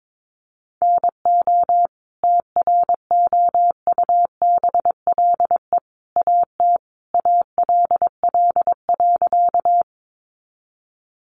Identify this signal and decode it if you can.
Morse code, heard as a keyed tone, about 22 wpm, 700 Hz. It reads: NO TROUBLE AT ALL.
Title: Sentences from Top 1200 Words 22wpm